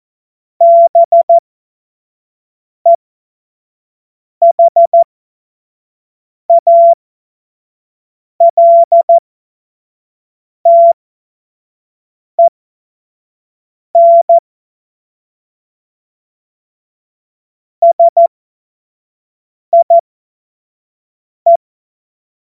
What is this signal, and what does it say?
Morse code, heard as a keyed tone, about 14 wpm, spaced as for 5 wpm, 680 Hz. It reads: BEHALTEN SIE
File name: Morsekurs / Witze / Witz 23 April 2024 25 F70